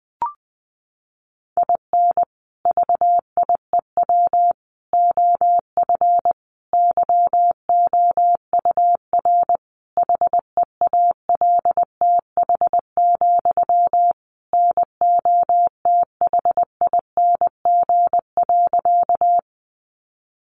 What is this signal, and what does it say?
Morse code, heard as a keyed tone, about 20 wpm, 700 Hz. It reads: IN VIEW OF YOUR HEALTH, NOTHING.
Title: Sentences from Top 900 Words 20wpm